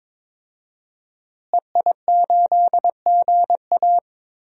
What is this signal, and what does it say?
Morse code, heard as a keyed tone, about 22 wpm, 700 Hz. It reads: EI8GA
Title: DXCC-100 Call Signs - 5 Characters - Repeated 3x - 22wpm